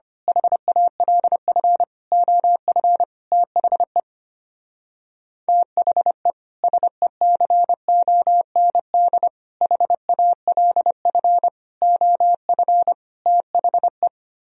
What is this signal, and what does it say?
Morse code, heard as a keyed tone, about 25 wpm, 700 Hz. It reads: HALF OF THE THE SECOND HALF OF THE
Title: Speed-Racing - Sets of 5 Words 25wpm